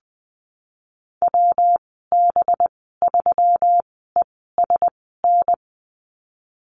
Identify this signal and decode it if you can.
Morse code, heard as a keyed tone, about 20 wpm, spaced as for 15 wpm, 700 Hz. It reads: WB3ESN
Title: 200 Lesson 32g - Letter k - Call Signs 20wpm